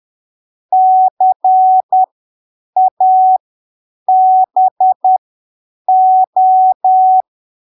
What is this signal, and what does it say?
Morse code, heard as a keyed tone, about 10 wpm, spaced as for 7 wpm, 750 Hz. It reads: CABO
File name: arquivos / texto / texto 16 10ppm